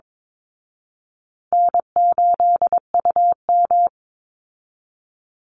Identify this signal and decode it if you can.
Morse code, heard as a keyed tone, about 22 wpm, 700 Hz. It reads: N8UM
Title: Easy Call Signs Repeated 3x - 22wpm